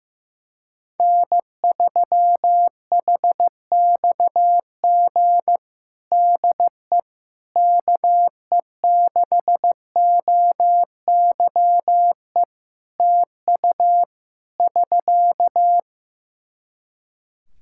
Morse code, heard as a keyed tone, about 15 wpm, 700 Hz. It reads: N3HXG DE KE6OYE TU <SK>
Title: Practice QSO - Exchange 4 - Ending - 15wpm